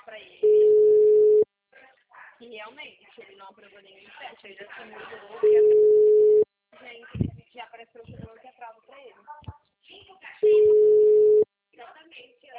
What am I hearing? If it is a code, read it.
Ringback tone (Europe, 425 Hz), with speech.